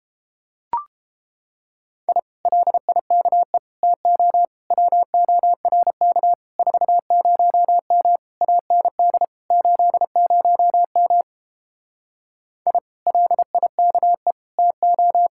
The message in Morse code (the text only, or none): I LIKE TO WORK 40M AND 80M I LIKE TO